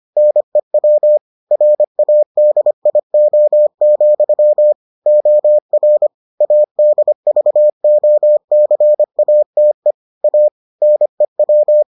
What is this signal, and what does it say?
Morse code, heard as a keyed tone, about 25 wpm, 600 Hz. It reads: NEW RADIO, OR ADVOCATE A NEW